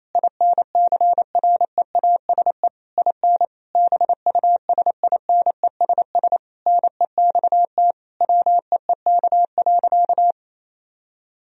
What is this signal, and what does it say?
Morse code, heard as a keyed tone, about 28 wpm, 700 Hz. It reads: INCREASE IN BUSINESS NEXT WEEK.
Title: Sentences from Top 1000 Words - Review Part 4 - 28wpm